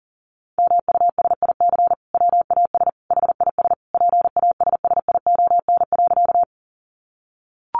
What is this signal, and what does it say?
Morse code, heard as a keyed tone, about 40 wpm, 700 Hz. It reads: MUSIC WAS HIS PASSION.
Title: Sentences from Top 1300 Words 40wpm